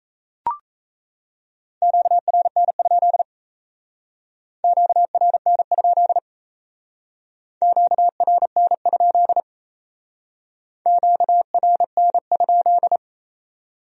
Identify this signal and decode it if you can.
Morse code, heard as a keyed tone, about 35 wpm, 700 Hz. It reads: QRN? QRN? QRN? QRN?
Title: Speed-Racing Q-Codes - 28wpm